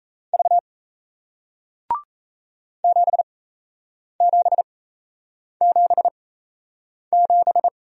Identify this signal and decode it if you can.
Morse code, heard as a keyed tone, about 38 wpm, spaced as for 8 wpm, 700 Hz. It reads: V 7777